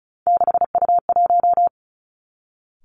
Morse code, heard as a keyed tone, about 36 wpm, 700 Hz. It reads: 6U1
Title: Three Challenging Letter-Number Combinations 35wpm